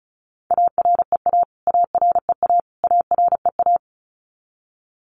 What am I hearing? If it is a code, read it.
Morse code, heard as a keyed tone, about 34 wpm, 700 Hz. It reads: AREA AREA AREA
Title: Top 500 Words Repeated 35wpm